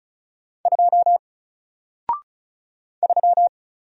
Morse code, heard as a keyed tone, about 36 wpm, 700 Hz. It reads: 2 3
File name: Single-Letter-Number-35wpm